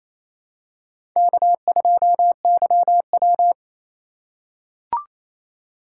Morse code, heard as a keyed tone, about 28 wpm, 700 Hz. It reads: K2YW
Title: USA Call Signs - 4x Character Spacing - 28wpm